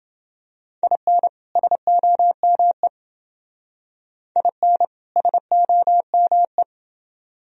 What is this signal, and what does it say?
Morse code, heard as a keyed tone, about 28 wpm, 700 Hz. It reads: IN SOME IN SOME